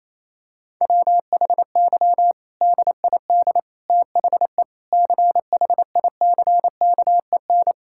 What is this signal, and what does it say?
Morse code, heard as a keyed tone, about 28 wpm, 700 Hz. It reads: WHY DID THE CHICKEN
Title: Sentences from Top 1000 Words - Review Part 5 - 28wpm